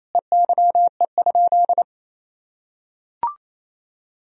Morse code, heard as a keyed tone, about 27 wpm, 700 Hz. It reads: EYE?
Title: Sentences from Top 1000 Words - Review Part 1 - 28wpm